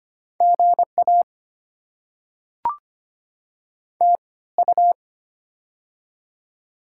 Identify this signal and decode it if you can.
Morse code, heard as a keyed tone, about 25 wpm, spaced as for 18 wpm, 700 Hz. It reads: GA TU
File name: Top 20 QSO Elements - 3x Character Spacing - 25wpm